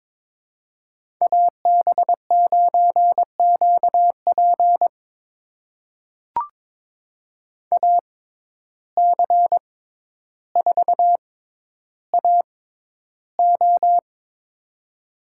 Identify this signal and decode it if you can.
Morse code, heard as a keyed tone, about 22 wpm, spaced as for 11 wpm, 700 Hz. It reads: AB9QP AC4AO
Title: USA Call Signs - 5 Characters - 6x Character Spacing - 22wpm